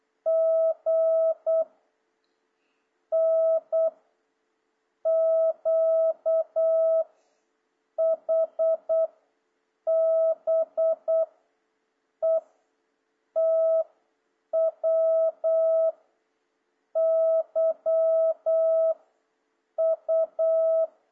Morse code, heard as a keyed tone, about 8 wpm, 660 Hz. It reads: G N Q H B E T W Y U
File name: jh8cbh / 2026cw